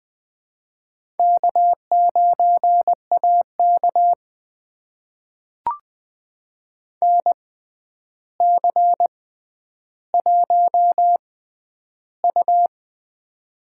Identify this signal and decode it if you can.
Morse code, heard as a keyed tone, about 21 wpm, spaced as for 10 wpm, 700 Hz. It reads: K9AK NC1U
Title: USA Call Signs - 4 Characters - 6x Character Spacing - 20wpm